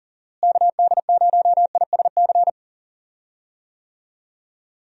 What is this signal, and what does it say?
Morse code, heard as a keyed tone, about 39 wpm, 700 Hz. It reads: KD0ISC
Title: USA Call Signs - 6 Characters - Repeated 3x - 40wpm